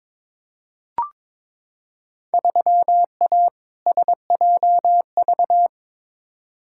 Morse code, heard as a keyed tone, about 22 wpm, 700 Hz. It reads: 3A SJV
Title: Field Day Exchanges Encore - 22wpm